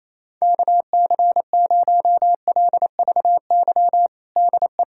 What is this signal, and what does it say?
Morse code, heard as a keyed tone, about 28 wpm, 700 Hz. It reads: KC0LVY DE
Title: USA Callsign DE Callsign 28wpm